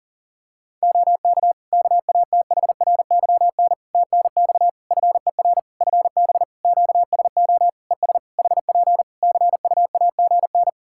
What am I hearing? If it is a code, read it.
Morse code, heard as a keyed tone, about 40 wpm, 700 Hz. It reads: OK KATHRYN TNX FER FB QSO ES HP CUAGN
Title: K1USN SST QSO Practice 40wpm